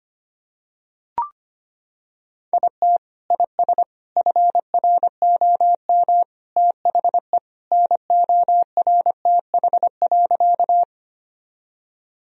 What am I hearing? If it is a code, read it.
Morse code, heard as a keyed tone, about 25 wpm, 700 Hz. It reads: IT IS FROM THE NORTH.
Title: Sentences from Top 1000 Words - Review Part 1 - 25wpm